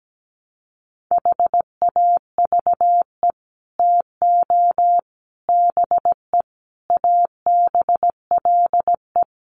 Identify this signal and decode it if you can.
Morse code, heard as a keyed tone, about 17 wpm, 700 Hz. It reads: HAVE TO BE ABLE
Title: Sets of 5 Words 17wpm